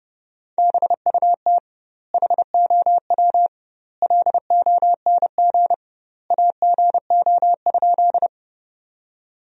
Morse code, heard as a keyed tone, about 30 wpm, 700 Hz. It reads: BUT HOW LONG AGO?
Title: 2x Word-Spacing - Sentences from Top 500 Words - 30wpm